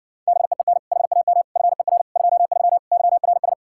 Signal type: Morse code

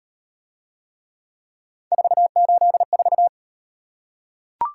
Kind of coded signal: Morse code

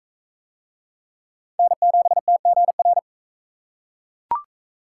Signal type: Morse code

